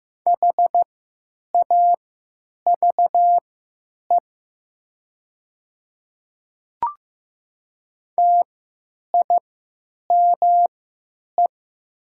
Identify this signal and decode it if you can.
Morse code, heard as a keyed tone, about 15 wpm, spaced as for 9 wpm, 700 Hz. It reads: HAVE TIME
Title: 3x Character Spacing - Top 200 Words - 15wpm